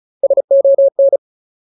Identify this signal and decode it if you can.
Morse code, heard as a keyed tone, about 34 wpm, 550 Hz. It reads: SON
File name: Ultimate Word List / 35WPM Ultimate